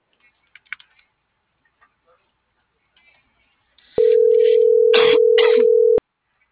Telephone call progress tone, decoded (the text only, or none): ringback tone